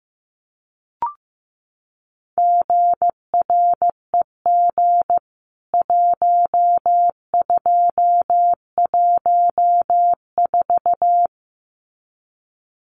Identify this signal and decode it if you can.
Morse code, heard as a keyed tone, about 15 wpm, 700 Hz. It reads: GREG 1214
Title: CWT Exchange 15wpm